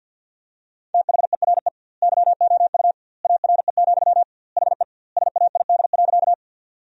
Morse code, heard as a keyed tone, about 50 wpm, 700 Hz. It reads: THERE YOU ARE, HE SAID.
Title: 2x Word-Spacing - Sentences from Top 100 Words - 50wpm